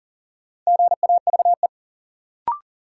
Morse code, heard as a keyed tone, about 40 wpm, 700 Hz. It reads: GAVE
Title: Sentences from Top 500 Words - New Word Drill - 40wpm